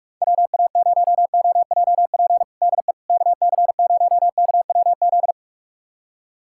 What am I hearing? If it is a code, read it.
Morse code, heard as a keyed tone, about 47 wpm, 700 Hz. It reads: WA0OJP DE KC0KWZ
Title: USA Callsign DE Callsign - 6 Characters - 45wpm